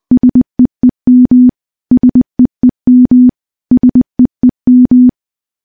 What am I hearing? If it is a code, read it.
Morse code, heard as a keyed tone, about 21 wpm, 250 Hz. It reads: SEEM SEEM SEEM
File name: morse / morse-training-20WPM